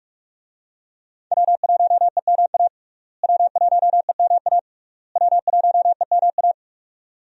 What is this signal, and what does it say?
Morse code, heard as a keyed tone, about 41 wpm, 700 Hz. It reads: W1EMA W1EMA W1EMA
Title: USA Call Signs - 5 Characters - Repeated 3x - 45wpm